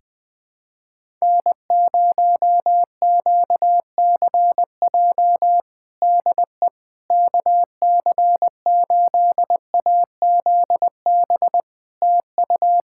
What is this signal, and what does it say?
Morse code, heard as a keyed tone, about 20 wpm, 700 Hz. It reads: N0QCJ DE KC8AZB TU
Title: Practice QSO - Exchange 4 - Ending - 20wpm